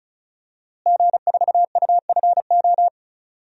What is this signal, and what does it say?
Morse code, heard as a keyed tone, about 36 wpm, 700 Hz. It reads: G4UFO
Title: DXCC-100 Call Signs - 5 Characters - Repeated 2x - 35wpm